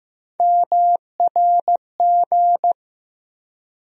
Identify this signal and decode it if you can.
Morse code, heard as a keyed tone, about 15 wpm, 700 Hz. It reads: MRG
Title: USA Call Sign Postfixes - 3 Letters 15wpm